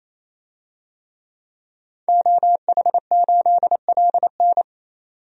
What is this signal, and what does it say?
Morse code, heard as a keyed tone, about 28 wpm, 700 Hz. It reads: OH8LN
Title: DXCC-100 Call Signs - 5 Characters - 28wpm